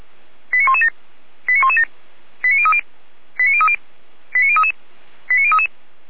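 Five-tone selective call (ZVEI), heard as quick runs of five tones, tones of about 70 ms. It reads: 89117 89117 89030 89030 89033 89033
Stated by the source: ZVEI five-tone selective call 89084 received with an SDR.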